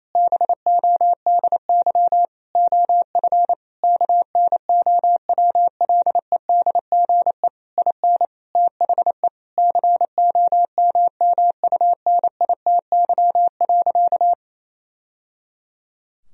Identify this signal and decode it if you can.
Morse code, heard as a keyed tone, about 28 wpm, 700 Hz. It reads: BODY OF KNOWLEDGE IN THE COMMUNITY.
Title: Sentences from Top 1000 Words - Review Part 4 - 28wpm